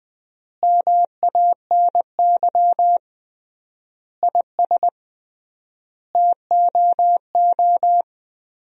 Morse code, heard as a keyed tone, about 20 wpm, 700 Hz. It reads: MANY IS TOO
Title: 3x Word-Spacing - Sentences from Top 400 Words - 20wpm